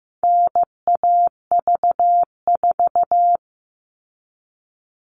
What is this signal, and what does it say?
Morse code, heard as a keyed tone, about 15 wpm, 700 Hz. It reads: NAV4
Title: All Cut Numbers - 4-Digit 15wpm